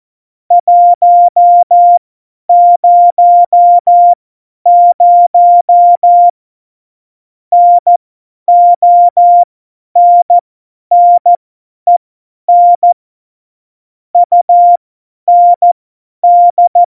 Morse code, heard as a keyed tone, about 14 wpm, spaced as for 10 wpm, 700 Hz. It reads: 100 NONNEN UND
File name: Morsekurs / Witze / Witz 12. Januar 2021, 50 F70